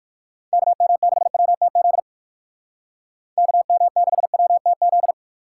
Morse code, heard as a keyed tone, about 60 wpm, 700 Hz. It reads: KM6WTZ KM6WTZ